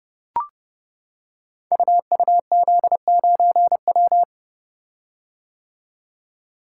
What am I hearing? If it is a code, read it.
Morse code, heard as a keyed tone, about 30 wpm, 700 Hz. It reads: UUZ9W
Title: Code Groups - 5 Letter:Numbers - 30wpm